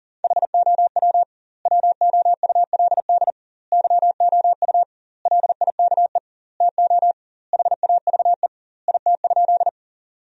Morse code, heard as a keyed tone, about 40 wpm, 700 Hz. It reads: HOW WOULD YOU LIKE TO HAVE IT?
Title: 2x Word-Spacing - Sentences from Top 100 Words - 40wpm